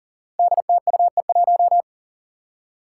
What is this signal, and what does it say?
Morse code, heard as a keyed tone, about 40 wpm, 700 Hz. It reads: DTUE1
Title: Five Challenging Letter-Number Combinations 40wpm